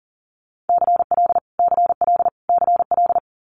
Morse code, heard as a keyed tone, about 40 wpm, 690 Hz. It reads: CL CL CL
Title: Multi-letter Phonemes Repeated 3x 40wpm